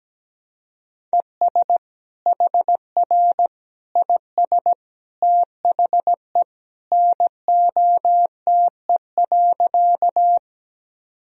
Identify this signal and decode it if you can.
Morse code, heard as a keyed tone, about 17 wpm, 700 Hz. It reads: ES HR IS THE NOTE.
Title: Sentences from Top 500 Words and Abbreviations 17wpm